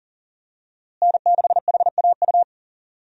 Morse code, heard as a keyed tone, about 40 wpm, 700 Hz. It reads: N6HAU
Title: USA Call Signs - 5 Characters - Repeated 2x - 40wpm